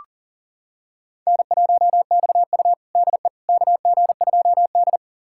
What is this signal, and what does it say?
Morse code, heard as a keyed tone, about 37 wpm, 700 Hz. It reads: N1XU DE KG2D
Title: USA Callsign DE Callsign - 4 Characters - 40wpm